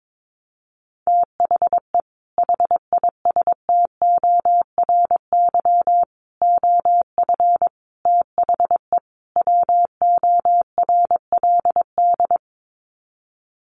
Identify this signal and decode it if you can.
Morse code, heard as a keyed tone, about 22 wpm, 700 Hz. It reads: THE HISTORY OF THE WORLD
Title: Sets of 5 Words Encore2 22wpm